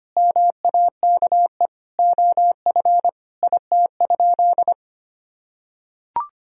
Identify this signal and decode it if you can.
Morse code, heard as a keyed tone, about 25 wpm, 700 Hz. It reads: MAKE OF IT?
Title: Sentences from Top 100 Words and Abbreviations 25wpm